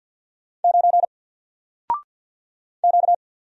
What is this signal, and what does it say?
Morse code, heard as a keyed tone, about 50 wpm, 700 Hz. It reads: 9 Q
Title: Single Letter-Number 50wpm